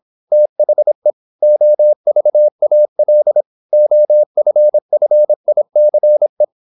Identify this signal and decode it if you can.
Morse code, heard as a keyed tone, about 26 wpm, 600 Hz. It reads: THE OVAL OFFICE
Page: Never 26wpm | CW med Gnister